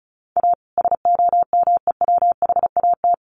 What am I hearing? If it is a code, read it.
Morse code, heard as a keyed tone, about 34 wpm, 700 Hz. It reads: A SOMEWHAT
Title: Sentences from Top 1300 Words 35wpm